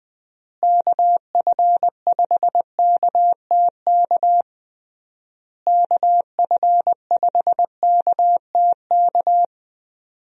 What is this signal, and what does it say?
Morse code, heard as a keyed tone, about 20 wpm, 700 Hz. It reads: KF5KTK KF5KTK
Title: USA Call Signs - 6 Characters - Repeated 3x - 20wpm